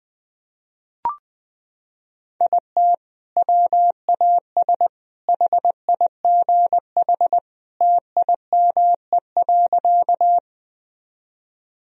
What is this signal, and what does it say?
Morse code, heard as a keyed tone, about 20 wpm, 700 Hz. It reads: IT WAS HIGH TIME.
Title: Sentences from Top 1000 Words - Review Part 1 - 20wpm